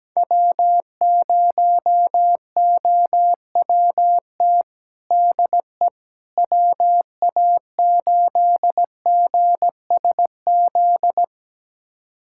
Morse code, heard as a keyed tone, about 17 wpm, 700 Hz. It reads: W0OWT DE WA8GSZ
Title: USA Callsign DE Callsign 17wpm